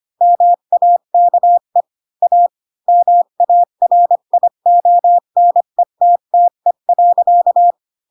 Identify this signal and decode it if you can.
Morse code, heard as a keyed tone, about 25 wpm, 700 Hz. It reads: MAKE A MARIONETTE.